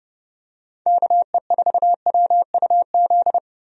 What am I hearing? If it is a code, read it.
Morse code, heard as a keyed tone, about 30 wpm, 700 Hz. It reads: KE4WUZ
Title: USA Call Signs - 6 Characters - Repeated 3x - 30wpm